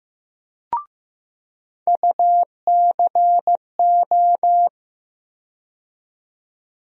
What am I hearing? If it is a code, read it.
Morse code, heard as a keyed tone, about 15 wpm, 700 Hz. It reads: UCO
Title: USA Call Sign Postfixes - 3 Letters 15wpm